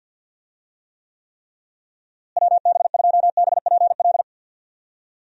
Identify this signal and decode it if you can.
Morse code, heard as a keyed tone, about 50 wpm, 700 Hz. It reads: WB2BPL
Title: USA Call Signs - 6 Characters - Repeated 3x - 50wpm